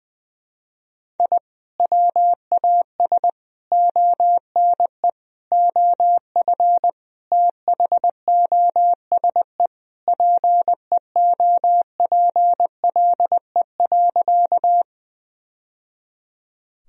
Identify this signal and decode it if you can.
Morse code, heard as a keyed tone, about 20 wpm, 700 Hz. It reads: I WAS ONE OF THOSE PEOPLE.
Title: Sentences from Top 1000 Words - Review Part 1 - 20wpm